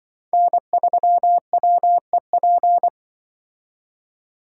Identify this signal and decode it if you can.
Morse code, heard as a keyed tone, about 24 wpm, 700 Hz. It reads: N3WEP
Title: Speed Racing - USA Call Signs 20wpm